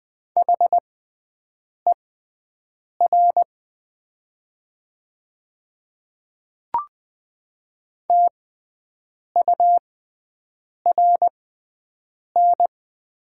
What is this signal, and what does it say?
Morse code, heard as a keyed tone, about 20 wpm, spaced as for 7 wpm, 700 Hz. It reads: HER TURN
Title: 6x Character Spacing - Top 200 Words - 20wpm